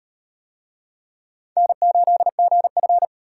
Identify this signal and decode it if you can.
Morse code, heard as a keyed tone, about 36 wpm, 700 Hz. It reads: N8GF